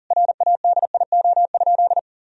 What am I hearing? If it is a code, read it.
Morse code, heard as a keyed tone, about 40 wpm, 700 Hz. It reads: RADIO?
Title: Sentences from Top 1000 Words - Review Part 3 - 40wpm